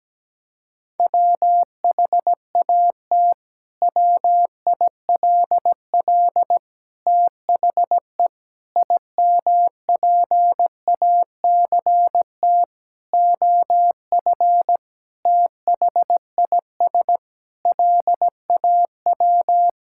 Morse code, heard as a keyed tone, about 17 wpm, 700 Hz. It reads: WHAT WILL THE IMPACT OF THIS LAW
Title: Sentences from Top 1000 Words - Review Part 6 - 17wpm